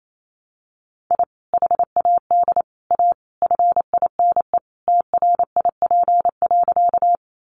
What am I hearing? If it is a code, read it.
Morse code, heard as a keyed tone, about 28 wpm, 700 Hz. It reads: I HAD A FINE TRIP.
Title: Sentences from Top 1300 Words 28wpm